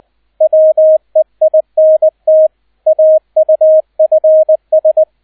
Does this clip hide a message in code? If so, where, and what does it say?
Yes: Morse code from 0.4 to 5.0 s, heard as a keyed tone, about 20 wpm, 630 Hz, with radio noise. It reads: WEINT AUFS